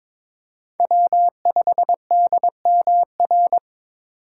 Morse code, heard as a keyed tone, about 22 wpm, 700 Hz. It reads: W5DMR
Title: USA Call Signs - 4x Character Spacing - 22wpm